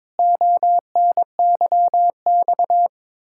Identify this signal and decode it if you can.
Morse code, heard as a keyed tone, about 22 wpm, 700 Hz. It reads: ONYX
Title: Pangram Challenge v2 - 22wpm